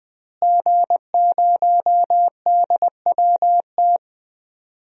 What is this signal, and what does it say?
Morse code, heard as a keyed tone, about 20 wpm, 700 Hz. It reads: G0DWT
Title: DXCC-100 Call Signs - 5 Characters - Repeated 2x - 20wpm